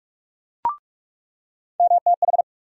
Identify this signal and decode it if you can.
Morse code, heard as a keyed tone, about 41 wpm, 700 Hz. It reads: MTH
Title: USA Call Sign Postfixes - 3 Letters 45wpm